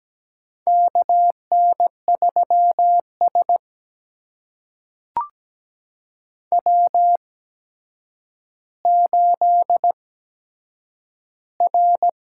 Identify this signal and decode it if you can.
Morse code, heard as a keyed tone, about 17 wpm, spaced as for 8 wpm, 700 Hz. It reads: KN3S W8R
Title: USA Call Signs - 4 Characters - 8x Character Spacing - 17wpm